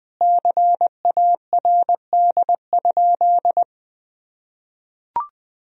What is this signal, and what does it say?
Morse code, heard as a keyed tone, about 20 wpm, 700 Hz. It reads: CARD?
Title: Sentences from Top 1000 Words - Review Part 4 - 20wpm